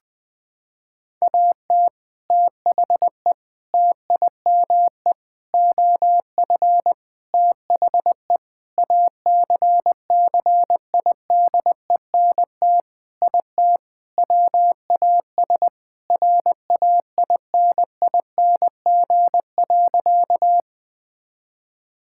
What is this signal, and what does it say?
Morse code, heard as a keyed tone, about 20 wpm, 700 Hz. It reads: AT THE TIME OF THE ACCIDENT IT WAS RAINING.
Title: Sentences with 5-Word Phrases - 20wpm